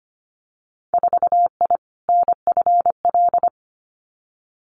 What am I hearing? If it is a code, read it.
Morse code, heard as a keyed tone, about 25 wpm, 700 Hz. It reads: 4I NFL
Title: Winter Field Day Exchanges 25wpm